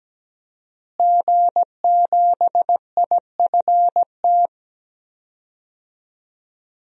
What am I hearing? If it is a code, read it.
Morse code, heard as a keyed tone, about 17 wpm, 700 Hz. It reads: G7IFT
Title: DXCC-100 Call Signs - 5 Characters - Repeated 3x - 17wpm